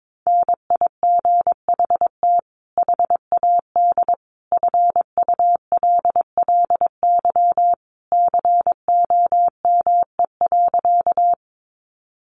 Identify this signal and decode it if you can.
Morse code, heard as a keyed tone, about 22 wpm, 700 Hz. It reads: NIGHT HAD FULLY COME.
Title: Sentences from Top 1300 Words 22wpm